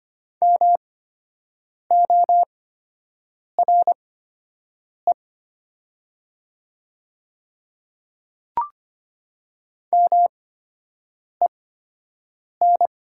Morse code, heard as a keyed tone, about 25 wpm, spaced as for 7 wpm, 690 Hz. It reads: MORE MEN
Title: 8x Character Spacing - Top 300 Words - 25wpm